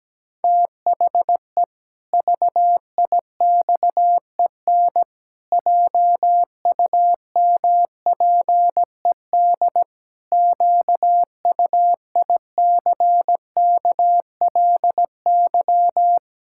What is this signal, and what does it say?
Morse code, heard as a keyed tone, about 17 wpm, 700 Hz. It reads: THE VIXEN JUMPED QUICKLY
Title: Pangram Challenge v2 - 17wpm